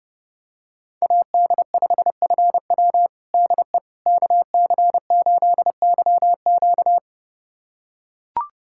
Morse code, heard as a keyed tone, about 30 wpm, 700 Hz. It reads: AD5FW DE KC8YQ
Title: USA Callsign DE Callsign - 5 Characters - 30wpm